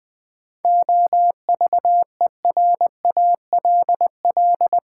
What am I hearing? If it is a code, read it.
Morse code, heard as a keyed tone, about 20 wpm, 700 Hz. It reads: OVERALL
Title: Sentences from Top 1000 Words - Review Part 7 - New Word Drill - 20wpm